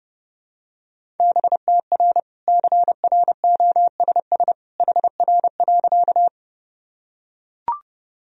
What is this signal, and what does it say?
Morse code, heard as a keyed tone, about 30 wpm, 700 Hz. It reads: BTR CROSS HR.
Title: Sentences from Top 400 Words and Abbreviations 30wpm